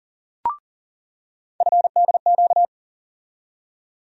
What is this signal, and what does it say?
Morse code, heard as a keyed tone, about 39 wpm, 700 Hz. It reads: FDQ
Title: USA Call Sign Postfixes - 3 Letters 40wpm